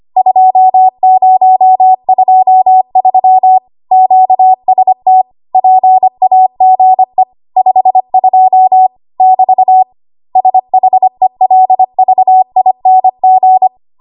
Morse code, heard as a keyed tone, about 25 wpm, 750 Hz. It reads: 2023 QST PAGE 52 <BT> SHELVING
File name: Morse / 25WPM